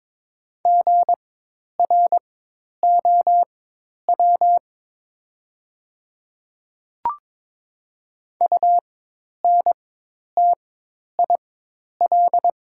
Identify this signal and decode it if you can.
Morse code, heard as a keyed tone, about 22 wpm, spaced as for 10 wpm, 700 Hz. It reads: GROW UNTIL
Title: 4x Character Spacing - Top 300 Words - 22wpm